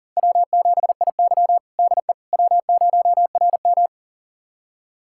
Morse code, heard as a keyed tone, about 40 wpm, 700 Hz. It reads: W7IY DE W0RM
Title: USA Callsign DE Callsign - 4 Characters - 40wpm